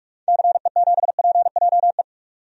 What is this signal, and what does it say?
Morse code, heard as a keyed tone, about 46 wpm, 700 Hz. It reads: CE7PJE
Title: DXCC-100 Call Signs - 6 Characters - 45wpm